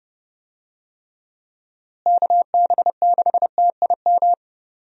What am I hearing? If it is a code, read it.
Morse code, heard as a keyed tone, about 30 wpm, 700 Hz. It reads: KB6TIM
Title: USA Call Signs - 6 Characters - Repeated 3x - 30wpm